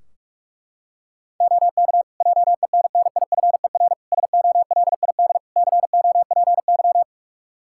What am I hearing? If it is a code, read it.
Morse code, heard as a keyed tone, about 46 wpm, 700 Hz. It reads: OK JENNIFER SOLID COPY
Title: Practice QSO - Exchange 3 - More Info - 45wpm